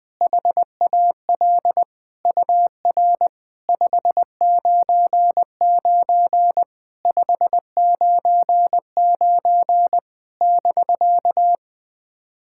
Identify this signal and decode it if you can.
Morse code, heard as a keyed tone, about 20 wpm, 700 Hz. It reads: HAL UR 599 599 <BK>
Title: SOTA QSO Practice 20wpm